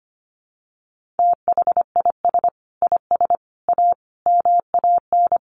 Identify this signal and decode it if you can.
Morse code, heard as a keyed tone, about 25 wpm, 700 Hz. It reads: THIS IS A MAN
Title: Sets of 5 Words Encore2 25wpm